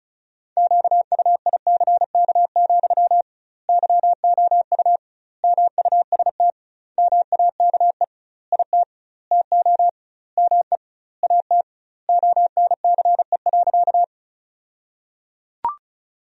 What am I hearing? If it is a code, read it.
Morse code, heard as a keyed tone, about 36 wpm, 700 Hz. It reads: QUICK, YOU MUST MAKE IT TO ME AT ONCE.
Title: 2x Word-Spacing - Sentences from Top 500 Words - 35wpm